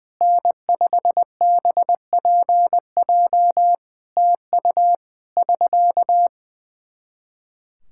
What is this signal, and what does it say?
Morse code, heard as a keyed tone, about 20 wpm, 700 Hz. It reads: N5BPJ TU <SK>
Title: K1USN SST QSO Practice 20wpm